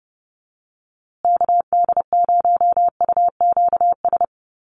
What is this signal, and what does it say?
Morse code, heard as a keyed tone, about 30 wpm, 700 Hz. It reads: KD0UQS
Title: USA Call Signs Repeated 3x 30wpm